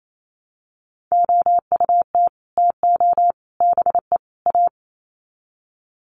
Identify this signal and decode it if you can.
Morse code, heard as a keyed tone, about 28 wpm, 700 Hz. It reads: OUT TO BE A
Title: Sets of 4 Words 28wpm